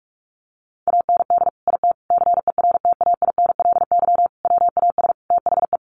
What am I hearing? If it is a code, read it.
Morse code, heard as a keyed tone, about 44 wpm, 700 Hz. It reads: AND IT CERTAINLY WAS THE
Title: Sentences from Top 1100 Words 45wpm